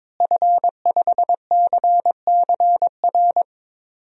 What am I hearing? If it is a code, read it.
Morse code, heard as a keyed tone, about 22 wpm, 700 Hz. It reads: F5CCR
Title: DXCC-100 Call Signs - 5 Characters - Repeated 3x - 22wpm